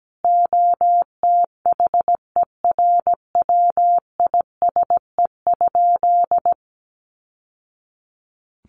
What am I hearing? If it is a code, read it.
Morse code, heard as a keyed tone, about 17 wpm, 700 Hz. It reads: OTHERWISE?
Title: Sentences from Top 1200 Words 17wpm